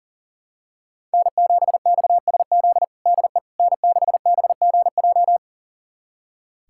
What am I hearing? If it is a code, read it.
Morse code, heard as a keyed tone, about 37 wpm, 700 Hz. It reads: N7XSZ DE N6BGJ
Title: USA Callsign DE Callsign - 5 Characters - 40wpm